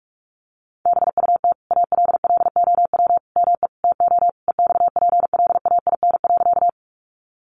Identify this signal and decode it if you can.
Morse code, heard as a keyed tone, about 44 wpm, 700 Hz. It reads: BUT ALLOW ME TO EXPLAIN.
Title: Sentences from Top 1100 Words 45wpm